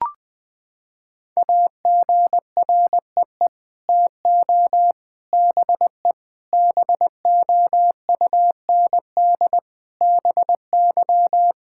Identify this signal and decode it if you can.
Morse code, heard as a keyed tone, about 21 wpm, 700 Hz. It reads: AGREE TO BE BOUND BY
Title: Sets of 9 Words - 20wpm